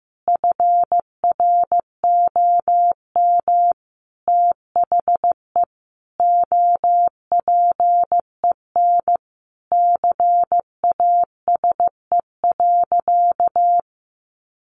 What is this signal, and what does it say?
Morse code, heard as a keyed tone, about 15 wpm, 700 Hz. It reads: FROM THE OPEN CASE.
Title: Sentences from Top 1000 Words 15wpm